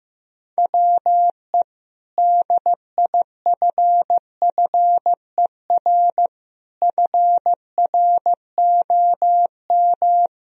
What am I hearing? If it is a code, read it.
Morse code, heard as a keyed tone, about 15 wpm, 700 Hz. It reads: WE DIFFER FROM
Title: Sentences from Top 1000 Words - Review Part 1 - 15wpm